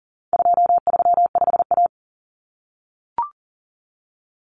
Morse code, heard as a keyed tone, about 40 wpm, 700 Hz. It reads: 235A
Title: Common Cut Numbers - 4-Digit 40wpm